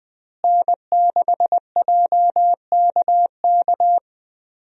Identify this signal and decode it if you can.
Morse code, heard as a keyed tone, about 20 wpm, 700 Hz. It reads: N6JKK
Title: USA Call Signs - 5 Characters - Repeated 3x - 20wpm